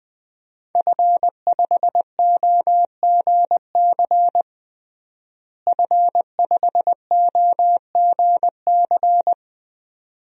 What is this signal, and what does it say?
Morse code, heard as a keyed tone, about 21 wpm, 700 Hz. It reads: F5OGC F5OGC
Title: DXCC-100 Call Signs - 5 Characters - Repeated 2x - 20wpm